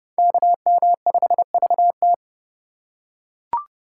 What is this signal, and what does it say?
Morse code, heard as a keyed tone, about 29 wpm, 700 Hz. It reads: KM5VT